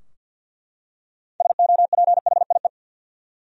Morse code, heard as a keyed tone, about 51 wpm, 700 Hz. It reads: SOPHIE
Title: USA Common Names - 50wpm